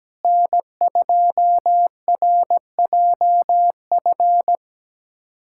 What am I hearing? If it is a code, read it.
Morse code, heard as a keyed tone, about 17 wpm, 700 Hz. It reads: N2RJF
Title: USA Callsign DE Callsign - 5 Characters - 17wpm